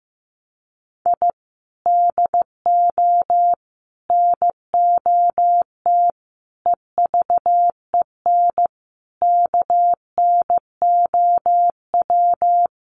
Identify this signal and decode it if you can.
Morse code, heard as a keyed tone, about 15 wpm, 700 Hz. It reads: I DO NOT EVEN KNOW
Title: Sets of 5 Words 15wpm